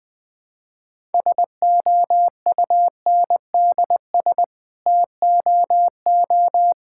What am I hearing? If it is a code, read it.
Morse code, heard as a keyed tone, about 21 wpm, 700 Hz. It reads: SOUNDS TOO
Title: Sets of 9 Words - 20wpm